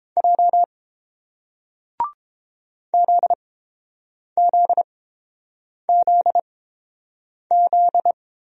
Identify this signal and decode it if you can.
Morse code, heard as a keyed tone, about 30 wpm, spaced as for 7 wpm, 700 Hz. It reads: J ZZZZ